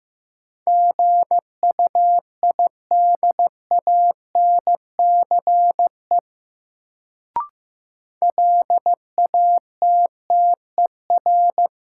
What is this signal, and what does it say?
Morse code, heard as a keyed tone, about 15 wpm, 700 Hz. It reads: GUIDANCE LATTER
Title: Sentences from Top 1600 Words - New Word Drill - 15wpm